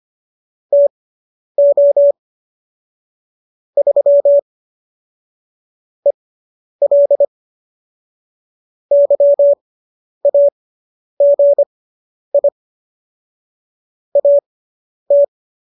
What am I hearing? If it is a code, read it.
Morse code, heard as a keyed tone, about 25 wpm, spaced as for 10 wpm, 570 Hz. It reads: TO 3 EL YAGI AT